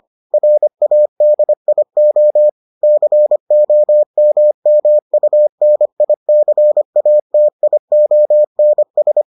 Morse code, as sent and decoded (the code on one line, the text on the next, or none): .-. .- -.. .. --- / -.-. --- -- -- ..- -. .. -.-. .- - .. --- -. ...
RADIO COMMUNICATIONS